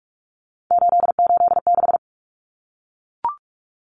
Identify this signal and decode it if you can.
Morse code, heard as a keyed tone, about 44 wpm, 700 Hz. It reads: 886
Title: Numbers - 3-Digit 45wpm